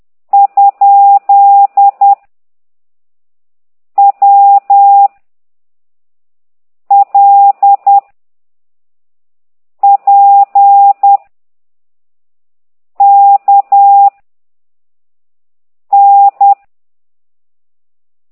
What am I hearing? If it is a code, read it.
Morse code, heard as a keyed tone, about 10 wpm, 800 Hz. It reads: ? W L P K N
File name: ct2idw / exerc9